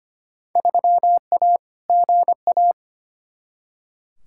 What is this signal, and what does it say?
Morse code, heard as a keyed tone, about 25 wpm, 700 Hz. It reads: 3A GA
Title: Field Day Exchanges - 25wpm